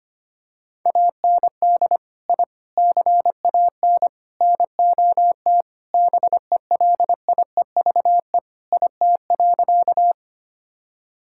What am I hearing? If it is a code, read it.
Morse code, heard as a keyed tone, about 25 wpm, 700 Hz. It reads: AND I CAN NOT BELIEVE IT.
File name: practice / 25wpm / Sentences with 4-Word Phrases - 25wpm